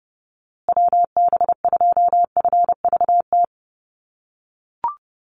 Morse code, heard as a keyed tone, about 30 wpm, 700 Hz. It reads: WB2FVT
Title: USA Call Signs Repeated 3x 30wpm